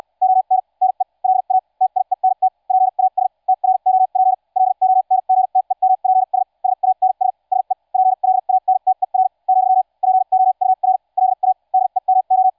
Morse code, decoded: ON NE LA TIENT PAS TROP LONGTEMPS SANS SE PLANTER , POUR LA DEMO